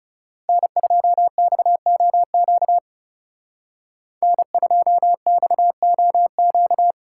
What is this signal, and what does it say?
Morse code, heard as a keyed tone, about 35 wpm, 700 Hz. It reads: N2XOQ N2XOQ
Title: Speed Racing - USA Call Signs 30wpm